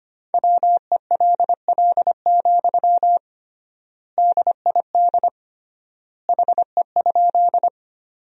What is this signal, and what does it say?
Morse code, heard as a keyed tone, about 25 wpm, 700 Hz. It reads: WELL, DID HE?
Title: 3x Word-Spacing - Sentences from Top 200 Words - 25wpm